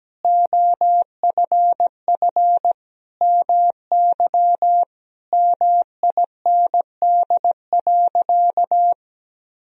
Morse code, heard as a keyed tone, about 17 wpm, 700 Hz. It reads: OFF MY MIND.
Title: Sentences from Top 1000 Words - Review Part 3 - 17wpm